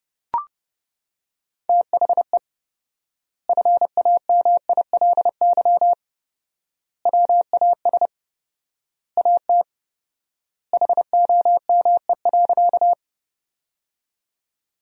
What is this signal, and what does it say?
Morse code, heard as a keyed tone, about 31 wpm, 700 Hz. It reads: THE FAMILY WAS AT HOME.
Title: 4x Word-Spacing - Sentences from Top 400 Words - 30wpm